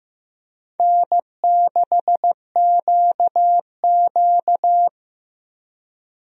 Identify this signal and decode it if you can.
Morse code, heard as a keyed tone, about 15 wpm, 700 Hz. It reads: N6QQ
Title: USA Call Signs - 4 Characters - Repeated 2x - 15wpm